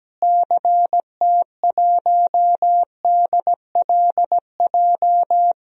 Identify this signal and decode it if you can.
Morse code, heard as a keyed tone, about 17 wpm, 700 Hz. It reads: CT1DLJ
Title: DXCC-100 Call Signs - 6 Characters - Repeated 3x - 17wpm